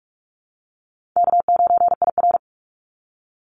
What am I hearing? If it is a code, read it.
Morse code, heard as a keyed tone, about 46 wpm, 700 Hz. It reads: K9IR
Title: NA SOTA Activator Callsigns 45wpm